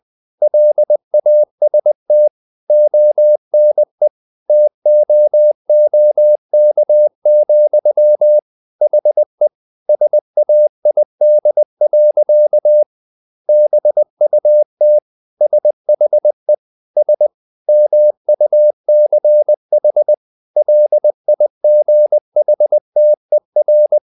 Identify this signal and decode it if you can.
Morse code, heard as a keyed tone, about 20 wpm, 600 Hz. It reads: LAST ONE TOOK, HE SAID. BUT SHE S MUCH LIGHTER
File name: the old man and the sea part 2 20wpm